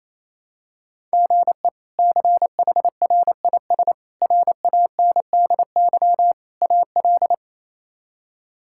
Morse code, heard as a keyed tone, about 27 wpm, 700 Hz. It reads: GE CHRIS RANDY AL
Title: K1USN SST QSO Practice - 28wpm